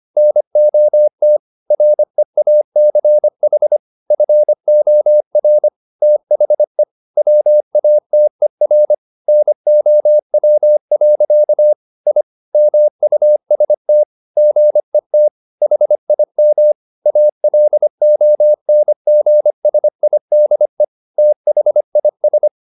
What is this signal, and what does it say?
Morse code, heard as a keyed tone, about 25 wpm, 600 Hz. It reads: NOT REACH FOR THE WATER NOW. I MUST GET HIM ALONGSIDE THIS